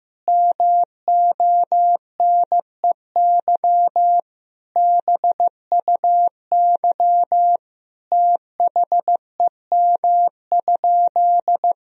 Morse code, heard as a keyed tone, about 16 wpm, 700 Hz. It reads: MONEY BUY THEM?
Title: Sentences from Top 1000 Words - Review Part 4 - 15wpm